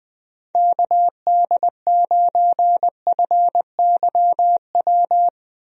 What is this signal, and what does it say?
Morse code, heard as a keyed tone, about 21 wpm, 700 Hz. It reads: KD9FYW
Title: USA Call Signs - 6 Characters - 20wpm